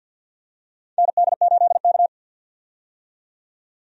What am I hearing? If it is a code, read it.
Morse code, heard as a keyed tone, about 48 wpm, 700 Hz. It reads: ND8K
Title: USA Call Signs - 4 Characters - 50wpm